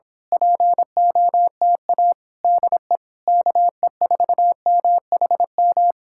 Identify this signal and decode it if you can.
Morse code, heard as a keyed tone, about 26 wpm, 700 Hz. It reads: POTA DE KE4MHM
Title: Speed-Racing Common QSO Phrases - 22wpm